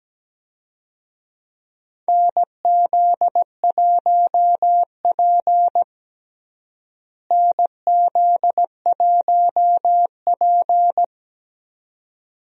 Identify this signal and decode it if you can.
Morse code, heard as a keyed tone, about 17 wpm, 700 Hz. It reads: NZ1P NZ1P
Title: USA Call Signs - 4 Characters - Repeated 2x - 17wpm